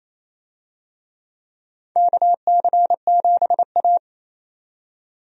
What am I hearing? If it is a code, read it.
Morse code, heard as a keyed tone, about 28 wpm, 700 Hz. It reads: KC7A
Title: USA Call Signs - 4 Characters - 28wpm